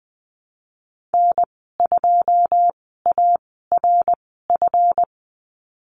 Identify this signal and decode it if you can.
Morse code, heard as a keyed tone, about 19 wpm, spaced as for 14 wpm, 700 Hz. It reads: N2ARF
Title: 154 Lesson 25g - Letter p - Call Signs 20wpm